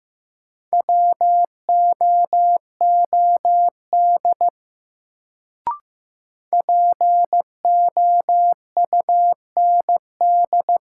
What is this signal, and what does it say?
Morse code, heard as a keyed tone, about 15 wpm, 700 Hz. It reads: WOOD POUND
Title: Sentences from Top 1000 Words - Review Part 2 - New Word Drill - 15wpm